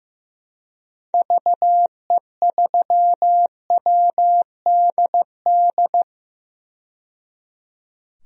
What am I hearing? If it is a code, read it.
Morse code, heard as a keyed tone, about 15 wpm, 700 Hz. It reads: VE3WDD
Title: DXCC-100 Call Signs - 6 Characters - 15wpm